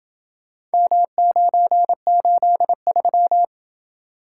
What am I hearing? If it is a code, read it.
Morse code, heard as a keyed tone, about 27 wpm, 700 Hz. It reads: M983